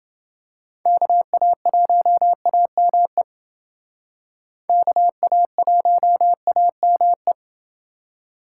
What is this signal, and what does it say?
Morse code, heard as a keyed tone, about 29 wpm, 700 Hz. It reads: KA1AME KA1AME